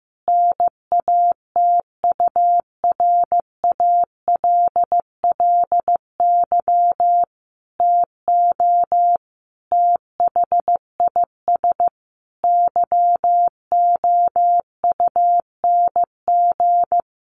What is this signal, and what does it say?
Morse code, heard as a keyed tone, about 15 wpm, 700 Hz. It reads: NATURALLY TO THIS YOUNG
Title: Sentences from Top 1300 Words 15wpm